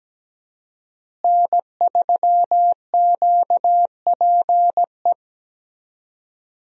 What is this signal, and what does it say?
Morse code, heard as a keyed tone, about 17 wpm, 700 Hz. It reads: N3QPE
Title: USA Call Signs - 5 Characters - Repeated 2x - 17wpm